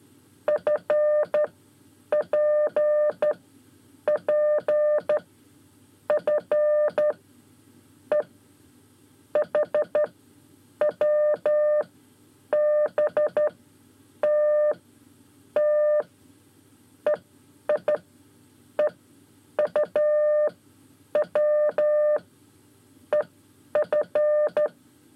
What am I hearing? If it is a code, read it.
Morse code, heard as a keyed tone, about 13 wpm, spaced as for 8 wpm, 600 Hz. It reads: FPP F E HWBT T EI EUW EF
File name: 5 Lezione CW 1